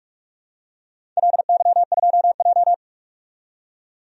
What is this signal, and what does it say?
Morse code, heard as a keyed tone, about 44 wpm, 700 Hz. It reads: LY2J
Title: DXCC-100 Call Signs - 4 Characters - 45wpm